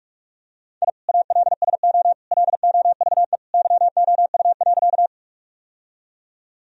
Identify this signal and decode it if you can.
Morse code, heard as a keyed tone, about 42 wpm, 700 Hz. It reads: I ALSO LOVE YOU.
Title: Sentences from Top 1000 Words - Review Part 1 - 45wpm